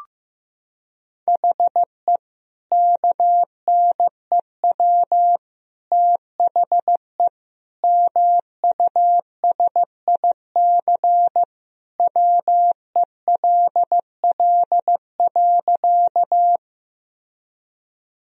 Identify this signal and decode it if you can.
Morse code, heard as a keyed tone, about 15 wpm, 700 Hz. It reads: HE KNEW THE MUSIC WELL.
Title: Sentences from Top 1000 Words - Review Part 1 - 15wpm